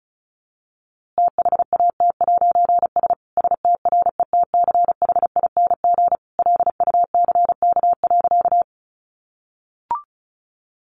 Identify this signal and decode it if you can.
Morse code, heard as a keyed tone, about 34 wpm, 700 Hz. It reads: THAT'S STRETCHING LUCK.
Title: A Sound Of Thunder 35wpm